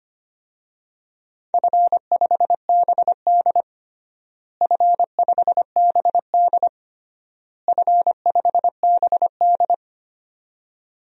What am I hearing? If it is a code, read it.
Morse code, heard as a keyed tone, about 25 wpm, 700 Hz. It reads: F5BD F5BD F5BD
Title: DXCC-100 Call Signs - 4 Characters - Repeated 3x - 25wpm